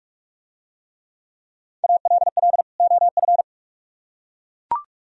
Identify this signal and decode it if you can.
Morse code, heard as a keyed tone, about 43 wpm, 700 Hz. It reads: ALL OF